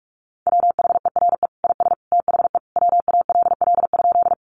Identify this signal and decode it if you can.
Morse code, heard as a keyed tone, about 44 wpm, 700 Hz. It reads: WHERE IS THE WALL?
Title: Sentences from Top 1100 Words 45wpm